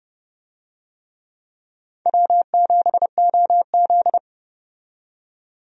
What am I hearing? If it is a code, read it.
Morse code, heard as a keyed tone, about 30 wpm, 700 Hz. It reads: W7OZ
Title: USA Call Signs - 4 Characters - 30wpm